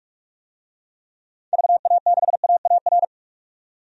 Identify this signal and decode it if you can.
Morse code, heard as a keyed tone, about 46 wpm, 700 Hz. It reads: VA6AAR
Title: DXCC-100 Call Signs - 6 Characters - 45wpm